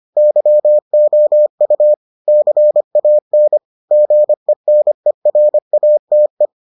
Morse code, heard as a keyed tone, about 25 wpm, 600 Hz. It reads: YOU CAN GENERATE